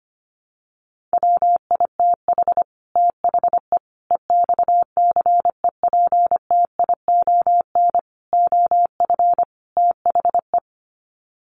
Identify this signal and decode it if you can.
Morse code, heard as a keyed tone, about 25 wpm, 690 Hz. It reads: WITH THE EXCEPTION OF THE
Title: Sets of 5 Words Encore 25wpm